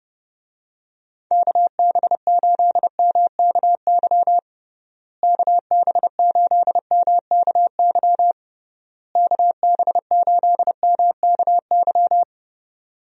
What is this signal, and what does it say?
Morse code, heard as a keyed tone, about 30 wpm, 700 Hz. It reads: KB8MKY KB8MKY KB8MKY
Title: USA Call Signs - 6 Characters - Repeated 3x - 30wpm